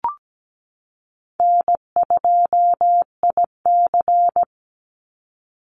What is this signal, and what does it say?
Morse code, heard as a keyed tone, about 17 wpm, 700 Hz. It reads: N2IC
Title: Easy Call Signs Repeated 3x - 17wpm